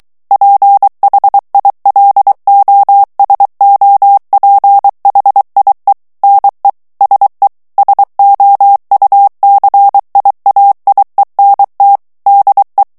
Morse code, heard as a keyed tone, about 23 wpm, 800 Hz. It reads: PHILOSOPHIE NE SE SOUCIAIENT DE
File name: FRANCAIS TEXTE CLAIR 2 VITESSE 22 MOTS MINUTES